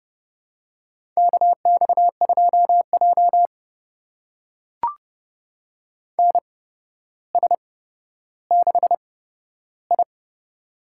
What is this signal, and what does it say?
Morse code, heard as a keyed tone, about 30 wpm, spaced as for 13 wpm, 700 Hz. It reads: KX2J NS6I
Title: USA Call Signs - 4 Characters - 8x Character Spacing - 30wpm